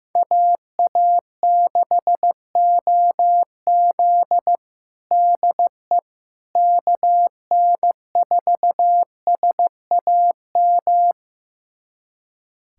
Morse code, heard as a keyed tone, about 15 wpm, 700 Hz. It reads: AA6OZ DE KN4SAM
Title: USA Callsign DE Callsign 15wpm